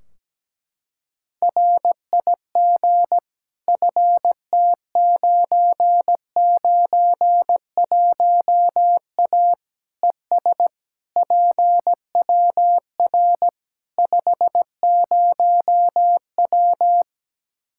Morse code, heard as a keyed tone, about 17 wpm, 700 Hz. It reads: RIG FT991A ES PWR 50W
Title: Practice QSO - Exchange 2 - Info - 17wpm